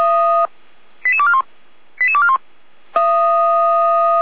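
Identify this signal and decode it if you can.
Five-tone selective call (ZVEI), heard as quick runs of five tones, tones of about 70 ms. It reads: 88241 88241
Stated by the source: ZVEI five-tone selective call received by SDR, recorded on the Feuerwehr Orlamünde audio server.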